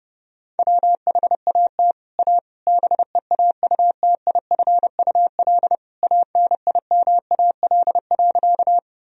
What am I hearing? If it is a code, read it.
Morse code, heard as a keyed tone, about 29 wpm, 700 Hz. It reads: WHAT A BEAUTIFUL ANIMAL.
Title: Sentences from Top 1000 Words - Review Part 1 - 30wpm